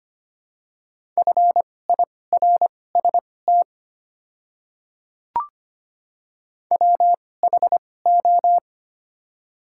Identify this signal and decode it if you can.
Morse code, heard as a keyed tone, about 25 wpm, spaced as for 18 wpm, 700 Hz. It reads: FIRST WHO
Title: 2x Character Spacing - Top 100 Words - 25wpm